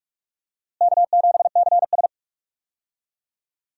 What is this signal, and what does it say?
Morse code, heard as a keyed tone, about 43 wpm, 700 Hz. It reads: K7CS
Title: USA Call Signs - 4 Characters - Repeated 2x - 45wpm